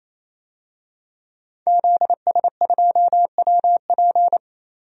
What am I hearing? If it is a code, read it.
Morse code, heard as a keyed tone, about 28 wpm, 700 Hz. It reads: ZS2WP
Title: DXCC-100 Call Signs - 5 Characters - Repeated 3x - 28wpm